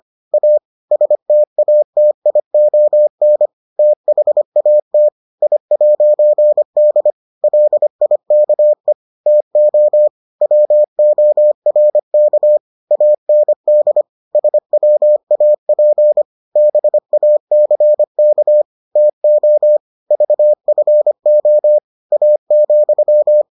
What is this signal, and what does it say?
Morse code, heard as a keyed tone, about 25 wpm, 600 Hz. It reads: A STATION THAT I'D LIKE TO WORK AND SWAP BACK TO VFO A,